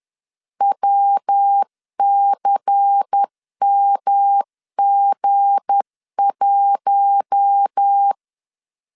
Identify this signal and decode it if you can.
Morse code, heard as a keyed tone, about 10 wpm, 800 Hz. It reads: WCMG1
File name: LEZIONI / 7 GF9Z / 7 G 10wpm 2501